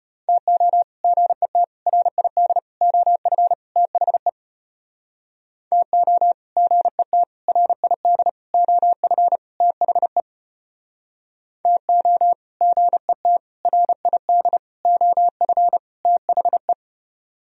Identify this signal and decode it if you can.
Morse code, heard as a keyed tone, about 35 wpm, 700 Hz. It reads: TO GET RID OF THE TO GET RID OF THE TO GET RID OF THE